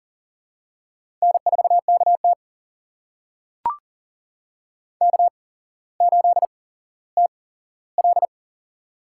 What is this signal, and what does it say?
Morse code, heard as a keyed tone, about 40 wpm, spaced as for 17 wpm, 700 Hz. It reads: N4KT K8TL
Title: USA Call Signs - 4 Characters - 8x Character Spacing - 40wpm